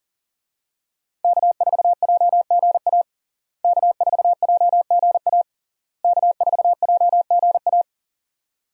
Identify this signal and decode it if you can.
Morse code, heard as a keyed tone, about 41 wpm, 700 Hz. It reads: K4JGA K4JGA K4JGA
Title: USA Call Signs - 5 Characters - Repeated 3x - 40wpm